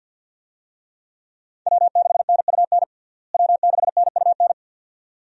Morse code, heard as a keyed tone, about 51 wpm, 700 Hz. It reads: W6NUN W6NUN
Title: USA Call Signs - 5 Characters - Repeated 2x - 50wpm